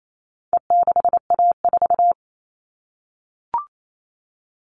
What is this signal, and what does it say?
Morse code, heard as a keyed tone, about 28 wpm, 700 Hz. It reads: E6A4
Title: All Cut Numbers - 4-Digit 28wpm